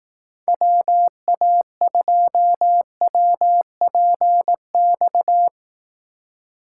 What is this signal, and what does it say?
Morse code, heard as a keyed tone, about 18 wpm, 700 Hz. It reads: WA2WPX
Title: Speed Racing - USA Call Signs 15wpm